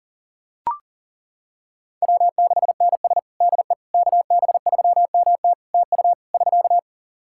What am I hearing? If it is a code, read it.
Morse code, heard as a keyed tone, about 40 wpm, 700 Hz. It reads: W6NS DE KB3MT TU <SK>
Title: Practice QSO - Exchange 4 - Ending - 40wpm